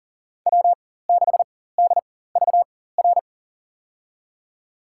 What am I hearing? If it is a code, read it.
Morse code, heard as a keyed tone, about 38 wpm, spaced as for 18 wpm, 700 Hz. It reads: W6DVR
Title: USA Call Signs - 5 Characters - 4x Character Spacing - 40wpm